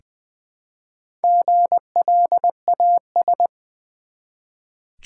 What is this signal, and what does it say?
Morse code, heard as a keyed tone, about 20 wpm, 700 Hz. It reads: GLAS
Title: Build A Word - 1 Letter At A Time - 5-Letter Words - 20wpm